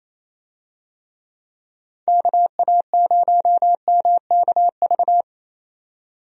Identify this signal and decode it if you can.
Morse code, heard as a keyed tone, about 28 wpm, 700 Hz. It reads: KA0MKV
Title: USA Call Signs - 6 Characters - 28wpm